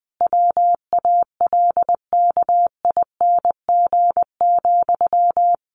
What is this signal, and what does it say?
Morse code, heard as a keyed tone, about 20 wpm, 700 Hz. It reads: WALKING,
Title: A Sound Of Thunder 20wpm